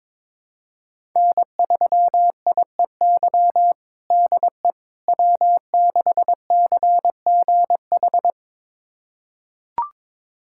Morse code, heard as a keyed tone, about 21 wpm, 700 Hz. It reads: N3IEY DE W6CGH
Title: USA Callsign DE Callsign - 5 Characters - 22wpm